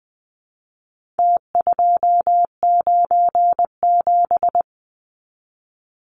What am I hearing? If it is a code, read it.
Morse code, heard as a keyed tone, about 20 wpm, 700 Hz. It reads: T297
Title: Numbers - Contest Style - 4-Digit 20wpm